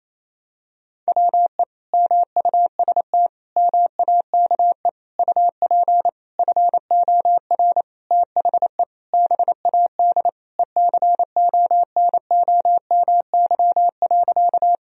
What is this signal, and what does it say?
Morse code, heard as a keyed tone, about 28 wpm, 700 Hz. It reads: WE MUST MAKE UP FOR THE BAD ECONOMY.
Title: Sentences from Top 1000 Words - Review Part 3 - 28wpm